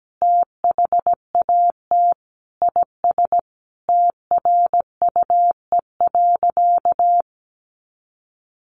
Morse code, heard as a keyed tone, about 17 wpm, 700 Hz. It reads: THAT IS TRUE.
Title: Sentences from Top 500 Words 17wpm